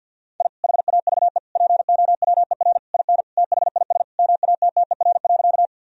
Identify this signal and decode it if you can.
Morse code, heard as a keyed tone, about 50 wpm, 700 Hz. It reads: I HAVE POWER IN THIS MATTER.
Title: Sentences from Top 1000 Words - Review Part 2 - 50wpm